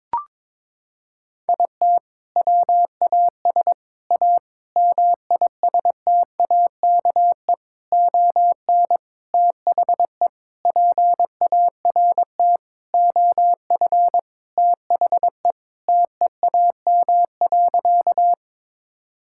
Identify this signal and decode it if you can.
Morse code, heard as a keyed tone, about 21 wpm, 700 Hz. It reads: IT WAS A MISTAKE ON THE PART OF THE TEAM.
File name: practice / 22wpm / Sentences with 4-Word Phrases - 22wpm